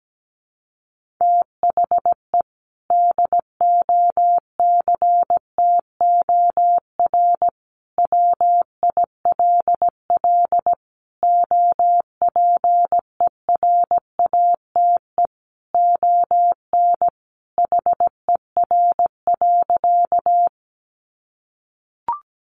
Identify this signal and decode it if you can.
Morse code, heard as a keyed tone, about 17 wpm, 700 Hz. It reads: THE DOCTOR WILL OPERATE ON HER.
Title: Sentences from Top 1500 Words 17wpm